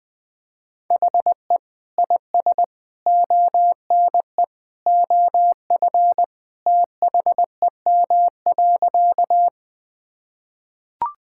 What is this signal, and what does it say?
Morse code, heard as a keyed tone, about 21 wpm, 700 Hz. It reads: HE IS ONE OF THEM.
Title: Sentences with 3-Word Phrases - 20wpm